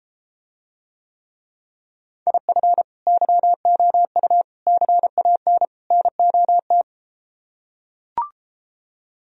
Morse code, heard as a keyed tone, about 31 wpm, 700 Hz. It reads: IF YOU CAN NOT